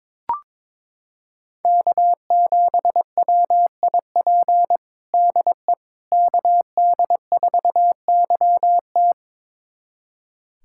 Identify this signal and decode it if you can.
Morse code, heard as a keyed tone, about 23 wpm, 700 Hz. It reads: K7WIP DE KD4YT
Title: USA Callsign DE Callsign - 5 Characters - 22wpm